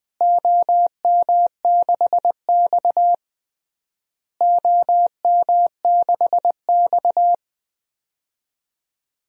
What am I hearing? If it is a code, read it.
Morse code, heard as a keyed tone, about 20 wpm, 700 Hz. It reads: OM6X OM6X
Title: DXCC-100 Call Signs - 4 Characters - Repeated 3x - 20wpm